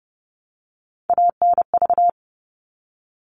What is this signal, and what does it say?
Morse code, heard as a keyed tone, about 30 wpm, 700 Hz. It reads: ANV
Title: All Cut Numbers - 3-Digit 30wpm